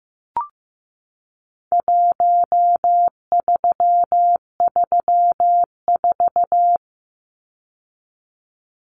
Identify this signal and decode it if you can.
Morse code, heard as a keyed tone, about 15 wpm, 700 Hz. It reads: 1334
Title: Numbers - Contest Style - 4-Digit 15wpm